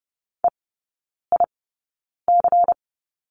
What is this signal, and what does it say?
Morse code, heard as a keyed tone, about 30 wpm, spaced as for 9 wpm, 700 Hz. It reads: EIC
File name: Lesson 16e - Letter c - Three Characters 30wpm